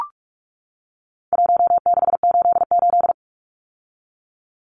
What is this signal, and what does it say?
Morse code, heard as a keyed tone, about 44 wpm, 700 Hz. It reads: 1688
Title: Numbers - Contest Style - 4-Digit 45wpm